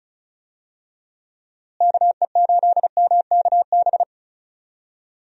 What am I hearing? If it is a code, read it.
Morse code, heard as a keyed tone, about 36 wpm, 700 Hz. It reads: KE8MKB
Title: USA Call Signs - 6 Characters - Repeated 2x - 35wpm